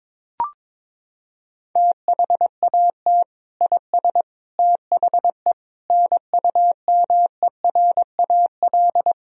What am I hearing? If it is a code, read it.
Morse code, heard as a keyed tone, about 22 wpm, 700 Hz. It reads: THAT IS THE NUMERAL
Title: Sentences from Top 1000 Words - Review Part 2 - 22wpm